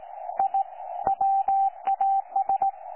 Morse code, heard as a keyed tone, about 17 wpm, 780 Hz, with radio noise. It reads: I WAS
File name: 12 39Z 18088.2kHz utah aa7fv k9tjl